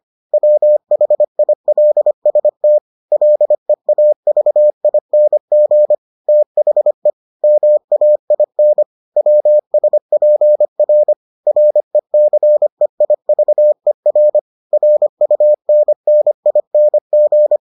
Morse code, heard as a keyed tone, about 25 wpm, 600 Hz. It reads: WHILST LEAVING THE MAIN WSPR RECEIVER RUNNING